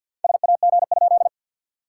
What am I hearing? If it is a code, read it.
Morse code, heard as a keyed tone, about 52 wpm, 690 Hz. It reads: SAG?
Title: SOTA QSO Practice 50wpm